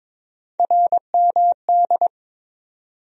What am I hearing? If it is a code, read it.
Morse code, heard as a keyed tone, about 22 wpm, 700 Hz. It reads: RMD
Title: USA Call Sign Postfixes - 3 Letters 22wpm